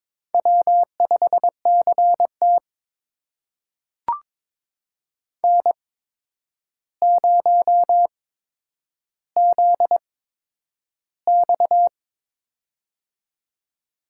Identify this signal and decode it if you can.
Morse code, heard as a keyed tone, about 22 wpm, spaced as for 9 wpm, 700 Hz. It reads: W5CT N0ZX
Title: USA Call Signs - 4 Characters - 8x Character Spacing - 22wpm